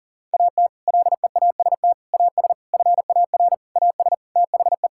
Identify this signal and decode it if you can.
Morse code, heard as a keyed tone, about 40 wpm, 690 Hz. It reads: AT LEAST AS FAR AS THE
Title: Sentences from Top 1000 Words - Review Part 6 - 40wpm